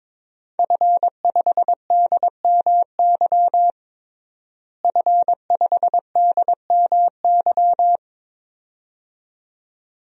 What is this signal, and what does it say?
Morse code, heard as a keyed tone, about 22 wpm, 700 Hz. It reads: F5DMY F5DMY
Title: DXCC-100 Call Signs - 5 Characters - Repeated 2x - 22wpm